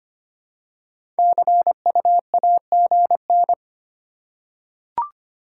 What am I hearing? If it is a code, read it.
Morse code, heard as a keyed tone, about 25 wpm, 700 Hz. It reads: CUAGN
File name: Top 20 QSO Elements - 3x Character Spacing - 25wpm